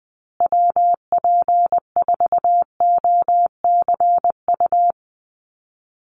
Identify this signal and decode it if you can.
Morse code, heard as a keyed tone, about 20 wpm, 700 Hz. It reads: WP4OCU
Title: USA Call Signs Repeated 3x 20wpm